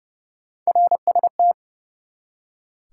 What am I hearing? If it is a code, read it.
Morse code, heard as a keyed tone, about 32 wpm, 700 Hz. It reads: RST
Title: Top 90 QSO Elements 30wpm